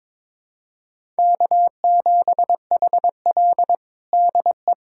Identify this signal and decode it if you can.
Morse code, heard as a keyed tone, about 21 wpm, 700 Hz. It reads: K7HL DE
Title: USA Callsign DE Callsign - 4 Characters - 22wpm